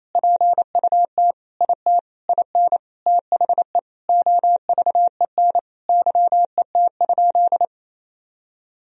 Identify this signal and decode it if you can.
Morse code, heard as a keyed tone, about 28 wpm, 700 Hz. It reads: PUT IT IN THE OVEN YET?
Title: Sentences from Top 1000 Words - Review Part 4 - 28wpm